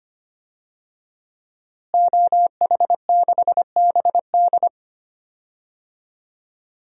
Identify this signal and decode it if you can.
Morse code, heard as a keyed tone, about 25 wpm, 700 Hz. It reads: OH6BD
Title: DXCC-100 Call Signs - 5 Characters - Repeated 3x - 25wpm